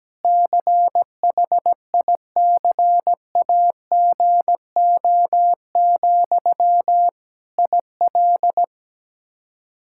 Morse code, heard as a keyed tone, about 17 wpm, 700 Hz. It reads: CHICAGO, IL
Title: K1USN SST QSO Practice 17wpm